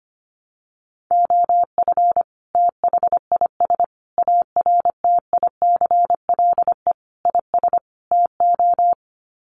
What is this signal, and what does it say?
Morse code, heard as a keyed tone, about 25 wpm, 700 Hz. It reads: OF THIS ARTICLE IS TO
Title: Sets of 5 Words Encore2 25wpm